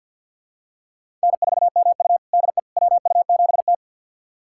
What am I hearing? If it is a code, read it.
Morse code, heard as a keyed tone, about 46 wpm, 700 Hz. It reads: N4MU DE WU7T
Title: USA Callsign DE Callsign - 4 Characters - 50wpm